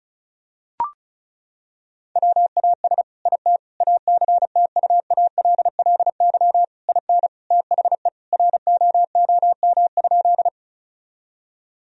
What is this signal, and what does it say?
Morse code, heard as a keyed tone, about 34 wpm, 700 Hz. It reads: WAS IT ACTUALLY IN THE ROOM?
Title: Sentences from Top 1000 Words - Review Part 4 - 35wpm